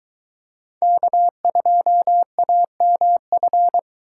Morse code, heard as a keyed tone, about 23 wpm, 700 Hz. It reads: K2AMF